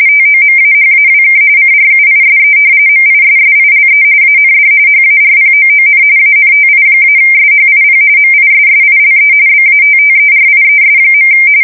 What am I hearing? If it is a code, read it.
RTTY, about 45.45 baud. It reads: TOMAHAWK CRUISE MISSILE AND HARPOON ANTI-SHIP MISSILE BATTERIES.
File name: fsk170 / battleshipnj